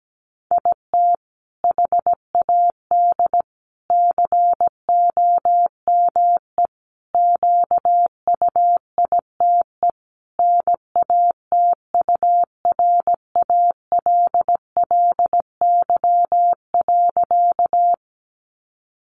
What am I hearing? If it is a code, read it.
Morse code, heard as a keyed tone, about 17 wpm, 700 Hz. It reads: IT HAD COME QUITE NATURALLY.
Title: Sentences from Top 1300 Words 17wpm